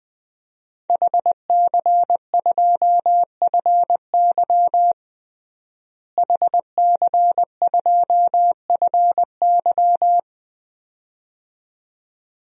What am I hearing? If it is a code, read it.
Morse code, heard as a keyed tone, about 20 wpm, 700 Hz. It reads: HC2FY HC2FY
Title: DXCC-100 Call Signs - 5 Characters - Repeated 2x - 20wpm